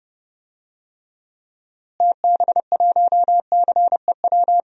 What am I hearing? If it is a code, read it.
Morse code, heard as a keyed tone, about 30 wpm, 700 Hz. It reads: TB1CEW
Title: DXCC-100 Call Signs - 6 Characters - Repeated 2x - 30wpm